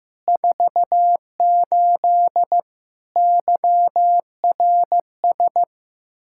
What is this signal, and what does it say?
Morse code, heard as a keyed tone, about 15 wpm, 700 Hz. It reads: 48 YRS
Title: Practice QSO - Exchange 3 - More Info - 15wpm